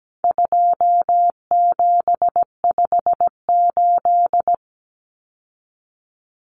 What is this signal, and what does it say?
Morse code, heard as a keyed tone, about 17 wpm, 700 Hz. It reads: 2758
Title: Numbers - 4-Digit 17wpm